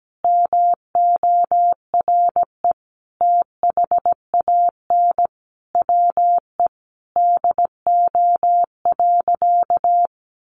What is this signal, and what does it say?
Morse code, heard as a keyed tone, about 17 wpm, 700 Hz. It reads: MORE THAN WE DO.
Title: Sentences from Top 100 Words 17wpm